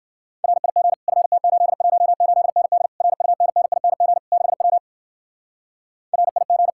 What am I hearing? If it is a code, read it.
Morse code, heard as a keyed tone, about 60 wpm, 700 Hz. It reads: RIG FT818ND RUNNING 6W RIG